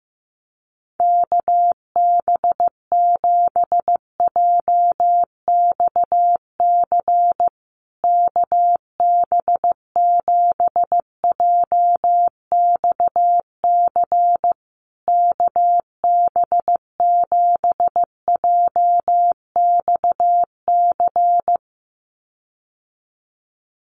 Morse code, heard as a keyed tone, about 14 wpm, 700 Hz. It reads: KB7JXC KB7JXC KB7JXC
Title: USA Call Signs Repeated 3x 15wpm